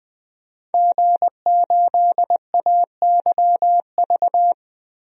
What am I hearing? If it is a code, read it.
Morse code, heard as a keyed tone, about 20 wpm, 700 Hz. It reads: G8AYV
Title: DXCC-100 Call Signs - 5 Characters - 20wpm